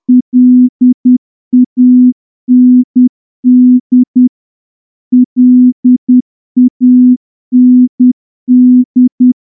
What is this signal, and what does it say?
Morse code, heard as a keyed tone, about 10 wpm, 250 Hz. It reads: LAND LAND
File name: morse-training-10WPM